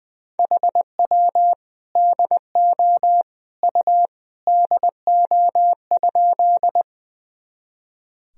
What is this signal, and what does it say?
Morse code, heard as a keyed tone, about 21 wpm, 700 Hz. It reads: HW DO U DO?
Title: Sentences from Top 100 Words and Abbreviations 20wpm